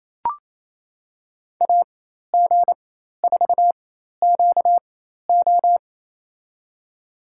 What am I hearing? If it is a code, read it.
Morse code, heard as a keyed tone, about 28 wpm, spaced as for 13 wpm, 700 Hz. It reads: AG4QO
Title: USA Call Signs - 5 Characters - 4x Character Spacing - 28wpm